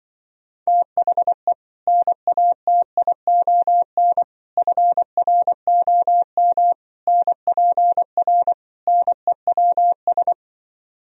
Morse code, heard as a keyed tone, about 24 wpm, 700 Hz. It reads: THE NATION FROM NPR NEWS
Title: Speed-Racing - Sets of 5 Words 20wpm